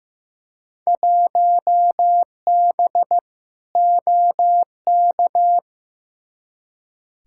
Morse code, heard as a keyed tone, about 15 wpm, 700 Hz. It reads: 1B OK
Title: Field Day Exchanges Encore - 15wpm